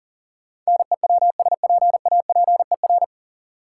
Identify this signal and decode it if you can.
Morse code, heard as a keyed tone, about 40 wpm, 700 Hz. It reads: NEWSPAPER
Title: Sentences from Top 1100 Words - New Word Drill Alt - 40wpm